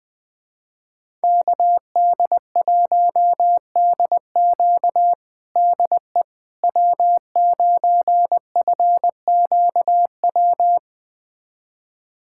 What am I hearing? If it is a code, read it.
Morse code, heard as a keyed tone, about 20 wpm, 700 Hz. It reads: KD1DQ DE W9FQW
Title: USA Callsign DE Callsign - 5 Characters - 20wpm